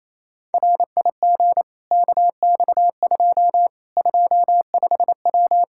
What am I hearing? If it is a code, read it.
Morse code, heard as a keyed tone, about 28 wpm, 700 Hz. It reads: RIG KX2 25W
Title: K1USN SST QSO Practice 28wpm